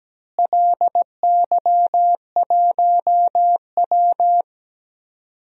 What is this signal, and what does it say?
Morse code, heard as a keyed tone, about 17 wpm, 700 Hz. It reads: LY1W
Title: DXCC-100 Call Signs - 4 Characters - Repeated 3x - 17wpm